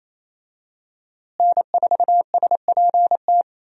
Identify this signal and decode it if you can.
Morse code, heard as a keyed tone, about 28 wpm, 700 Hz. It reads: N4SPT
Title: USA Call Signs - 5 Characters - Repeated 2x - 28wpm